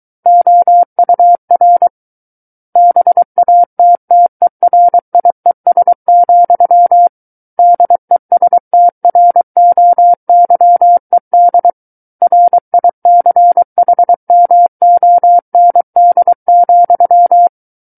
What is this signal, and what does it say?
Morse code, heard as a keyed tone, about 23 wpm, 700 Hz. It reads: OUR BATTERIES, DESTROYED RICHMOND,
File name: Book One / 14-Chapter 14 - 23 WPM